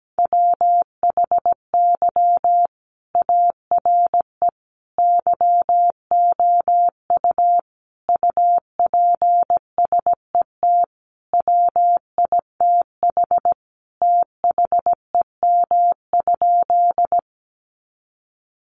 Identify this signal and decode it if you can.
Morse code, heard as a keyed tone, about 17 wpm, 700 Hz. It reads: WHY ARE YOU UPSET WITH THEM?
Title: Sentences from Top 800 Words 17wpm